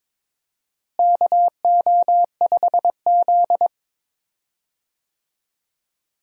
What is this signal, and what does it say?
Morse code, heard as a keyed tone, about 21 wpm, 700 Hz. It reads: KO5Z
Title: USA Call Signs - 4 Characters - Repeated 3x - 22wpm